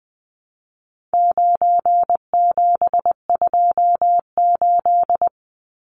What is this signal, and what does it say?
Morse code, heard as a keyed tone, about 20 wpm, 700 Hz. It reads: 9728
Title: Numbers - 4-Digit 20wpm